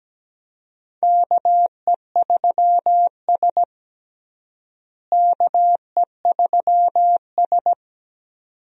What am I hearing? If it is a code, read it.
Morse code, heard as a keyed tone, about 17 wpm, 700 Hz. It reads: KE3S KE3S
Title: USA Call Signs - 4 Characters - Repeated 2x - 17wpm